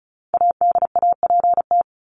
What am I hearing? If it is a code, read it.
Morse code, heard as a keyed tone, about 34 wpm, 700 Hz. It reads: ADAPT
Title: Top 1501-1750 Words 35wpm